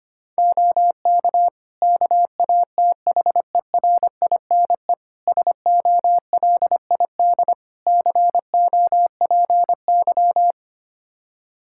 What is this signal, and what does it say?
Morse code, heard as a keyed tone, about 25 wpm, 700 Hz. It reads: OK KATHERINE SOLID COPY
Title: K1USN SST QSO Practice 25wpm